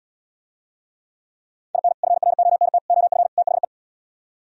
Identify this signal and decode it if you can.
Morse code, heard as a keyed tone, about 77 wpm, 700 Hz. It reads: IN FRONT OF THE